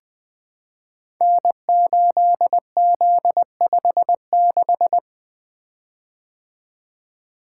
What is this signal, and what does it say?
Morse code, heard as a keyed tone, about 20 wpm, 700 Hz. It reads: N8Z56
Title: Code Groups - 5 Letter:Numbers - 20wpm